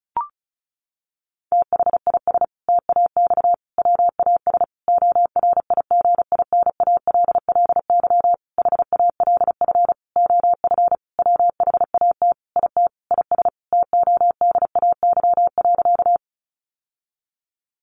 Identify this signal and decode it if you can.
Morse code, heard as a keyed tone, about 34 wpm, 700 Hz. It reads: THIS TAX WAS ORIGINALLY HALF OF WHAT IT IS TODAY.
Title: Sentences from Top 1500 Words 35wpm